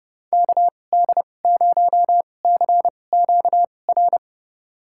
Morse code, heard as a keyed tone, about 29 wpm, spaced as for 22 wpm, 700 Hz. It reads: KD0CQR
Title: USA Call Signs - 6 Characters - 2x Character Spacing - 30wpm